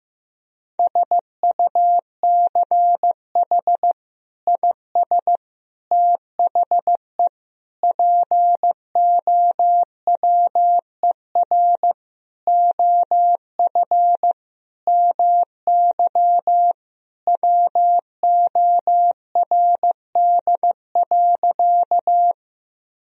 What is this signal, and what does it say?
Morse code, heard as a keyed tone, about 15 wpm, 700 Hz. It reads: SUCH IS THE POWER OF MY WORD.
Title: Sentences from Top 1000 Words - Review Part 2 - 15wpm